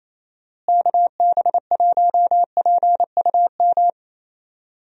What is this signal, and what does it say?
Morse code, heard as a keyed tone, about 28 wpm, 700 Hz. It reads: KB1PUM
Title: USA Call Signs - 6 Characters - Repeated 2x - 28wpm